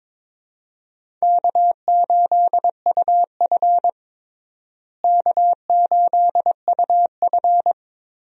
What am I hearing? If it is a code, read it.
Morse code, heard as a keyed tone, about 22 wpm, 700 Hz. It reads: K8UF K8UF
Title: USA Call Signs - 4 Characters - Repeated 2x - 22wpm